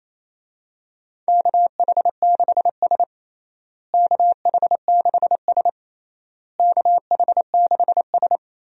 Morse code, heard as a keyed tone, about 28 wpm, 700 Hz. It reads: KH6S KH6S KH6S
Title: DXCC-100 Call Signs - 4 Characters - Repeated 3x - 28wpm